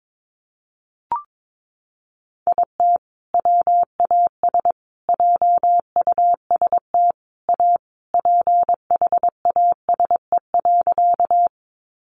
Morse code, heard as a keyed tone, about 22 wpm, 700 Hz. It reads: IT WAS JUST A PHASE.
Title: Sentences from Top 1200 Words 22wpm